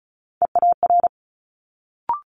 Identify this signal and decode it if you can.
Morse code, heard as a keyed tone, about 36 wpm, 700 Hz. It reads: EAR
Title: Multi-letter Phonemes Quick Fire 35wpm